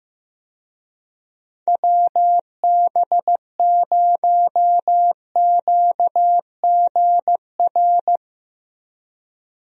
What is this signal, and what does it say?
Morse code, heard as a keyed tone, about 15 wpm, 700 Hz. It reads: WB0QGR
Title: USA Call Signs - 6 Characters - 15wpm